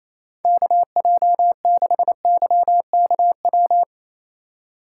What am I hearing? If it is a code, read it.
Morse code, heard as a keyed tone, about 28 wpm, 700 Hz. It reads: KJ6YKW
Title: USA Call Signs - 3x Character Spacing - 28wpm